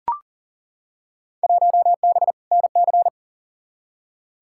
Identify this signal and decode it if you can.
Morse code, heard as a keyed tone, about 40 wpm, 700 Hz. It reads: 1B NC
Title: Field Day Exchanges Encore - 40wpm